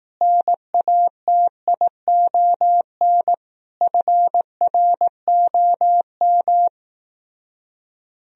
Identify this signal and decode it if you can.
Morse code, heard as a keyed tone, about 18 wpm, 700 Hz. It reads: NATION FROM
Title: Speed-Racing - Sets of 5 Words 15wpm